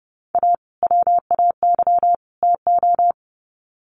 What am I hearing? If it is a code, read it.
Morse code, heard as a keyed tone, about 30 wpm, 700 Hz. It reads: A WAY TO
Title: Sets of 3 Words 30wpm